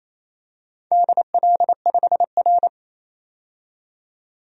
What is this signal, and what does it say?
Morse code, heard as a keyed tone, about 28 wpm, 700 Hz. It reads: DL5R
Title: DXCC-100 Call Signs - 4 Characters - 28wpm